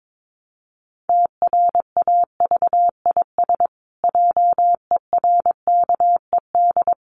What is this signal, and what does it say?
Morse code, heard as a keyed tone, about 22 wpm, 700 Hz. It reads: TRAVIS JERKED
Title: A Sound Of Thunder 22wpm